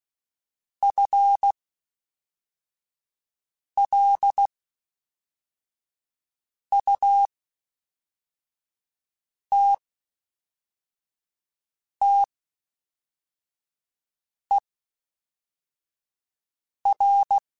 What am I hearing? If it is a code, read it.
Morse code, heard as a keyed tone, about 16 wpm, spaced as for 4 wpm, 800 Hz. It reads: FLUTTER
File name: Spaced / FL Runs sp mp3s / FL Runs sp @5